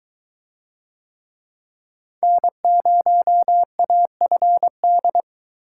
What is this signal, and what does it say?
Morse code, heard as a keyed tone, about 22 wpm, 700 Hz. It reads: N0AFD